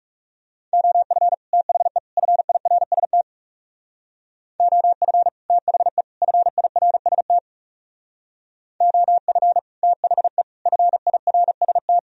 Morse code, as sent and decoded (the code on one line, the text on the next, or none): --- ..-. / - .... / ..-. .. .-. ... - / --- ..-. / - .... / ..-. .. .-. ... - / --- ..-. / - .... / ..-. .. .-. ... -
OF THE FIRST OF THE FIRST OF THE FIRST